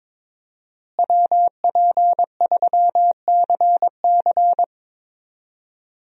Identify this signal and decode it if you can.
Morse code, heard as a keyed tone, about 22 wpm, 700 Hz. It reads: WP3CC
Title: USA Call Signs - 5 Characters - Repeated 2x - 22wpm